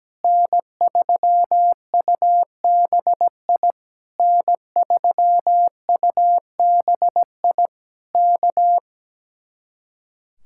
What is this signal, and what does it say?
Morse code, heard as a keyed tone, about 17 wpm, 700 Hz. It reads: N3UBI N3UBI K
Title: Speed-Racing Common QSO Phrases Encore - 17wpm